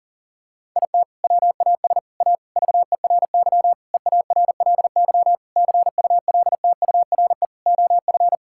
Morse code, heard as a keyed tone, about 40 wpm, 690 Hz. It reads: IT WAS A VERY EARLY CULTURE OF
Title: Sentences from Top 1000 Words - Review Part 6 - 40wpm